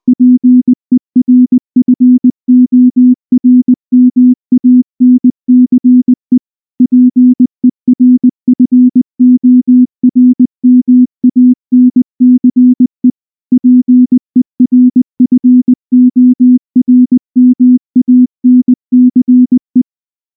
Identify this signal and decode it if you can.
Morse code, heard as a keyed tone, about 19 wpm, 250 Hz. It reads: PERFORMANCE PERFORMANCE PERFORMANCE
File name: morse / morse-training-20WPM-3